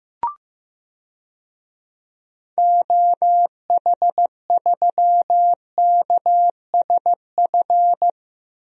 Morse code, heard as a keyed tone, about 15 wpm, 700 Hz. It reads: OH3KSF
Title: DXCC-100 Call Signs - 6 Characters - Repeated 2x - 15wpm